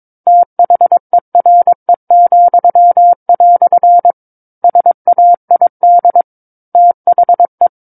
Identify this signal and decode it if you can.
Morse code, heard as a keyed tone, about 22 wpm, 700 Hz. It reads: THERE," SAID THE
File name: 06-Chapter-06 22 WPM